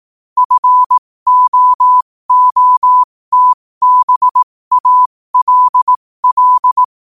Morse code, heard as a keyed tone, about 18 wpm, spaced as for 15 wpm, 1000 Hz. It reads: FOOTBALL
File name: qotdfiles / 15wpm